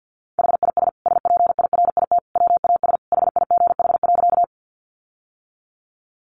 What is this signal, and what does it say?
Morse code, heard as a keyed tone, about 50 wpm, 700 Hz. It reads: HIS SPIRIT WAS HIGH.
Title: Sentences from Top 1400 Words 50wpm